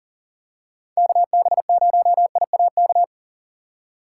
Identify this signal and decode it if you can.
Morse code, heard as a keyed tone, about 40 wpm, 700 Hz. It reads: KB0IAK
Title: USA Call Signs - 6 Characters - 2x Character Spacing - 40wpm